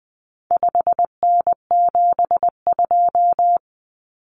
Morse code, heard as a keyed tone, about 20 wpm, 700 Hz. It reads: 5N72
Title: Common Cut Numbers - 4-Digit 20wpm